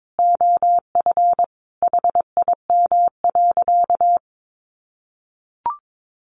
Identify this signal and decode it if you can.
Morse code, heard as a keyed tone, about 22 wpm, 700 Hz. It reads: OF HIM.
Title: Sentences from Top 1100 Words 22wpm